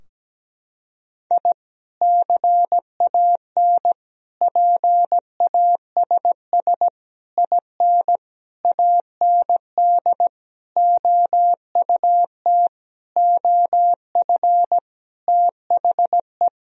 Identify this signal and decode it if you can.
Morse code, heard as a keyed tone, about 17 wpm, 700 Hz. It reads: I CAN PASS IN AND OUT OF THE
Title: Sentences from Top 1000 Words - Review Part 4 - 17wpm